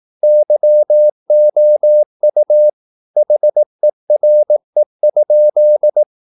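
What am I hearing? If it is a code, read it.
Morse code, heard as a keyed tone, about 18 wpm, 600 Hz. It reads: YOU HERE?
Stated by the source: Morse practice at 18 wpm, English text, from CW med Gnister.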